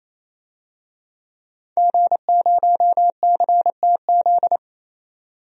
Morse code, heard as a keyed tone, about 28 wpm, 700 Hz. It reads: G0CTZ
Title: DXCC-100 Call Signs - 5 Characters - 28wpm